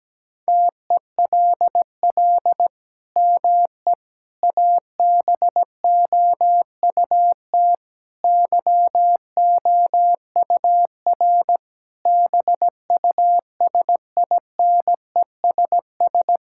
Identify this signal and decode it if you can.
Morse code, heard as a keyed tone, about 17 wpm, 700 Hz. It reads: TELL ME ABOUT YOUR BUSINESS
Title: Sentences from Top 1000 Words - Review Part 5 - 17wpm